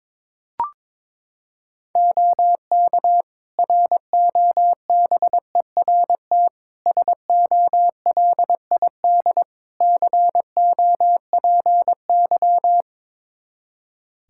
Morse code, heard as a keyed tone, about 21 wpm, 700 Hz. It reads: OK ROBERT SOLID COPY
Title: K1USN SST QSO Practice 22wpm